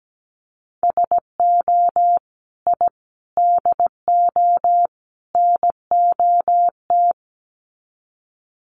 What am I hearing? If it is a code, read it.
Morse code, heard as a keyed tone, about 17 wpm, 700 Hz. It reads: SO I DO NOT
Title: Sets of 4 Words 17wpm